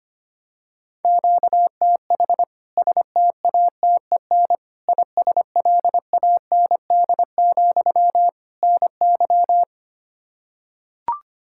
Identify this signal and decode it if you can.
Morse code, heard as a keyed tone, about 25 wpm, 700 Hz. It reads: QTH STATEN ISLAND, NY
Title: K1USN SST QSO Practice 25wpm